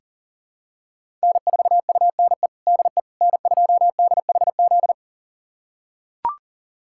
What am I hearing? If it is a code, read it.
Morse code, heard as a keyed tone, about 37 wpm, 700 Hz. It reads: N4UNE DE N2DHZ
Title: USA Callsign DE Callsign - 5 Characters - 40wpm